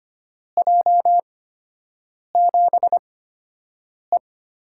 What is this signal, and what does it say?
Morse code, heard as a keyed tone, about 25 wpm, spaced as for 7 wpm, 700 Hz. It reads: J7E
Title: USA Call Signs - 6 Characters - 8x Character Spacing - 25wpm